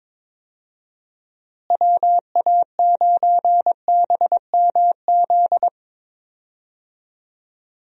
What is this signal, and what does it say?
Morse code, heard as a keyed tone, about 22 wpm, 700 Hz. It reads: WA9BMZ
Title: USA Call Signs - 6 Characters - Repeated 2x - 22wpm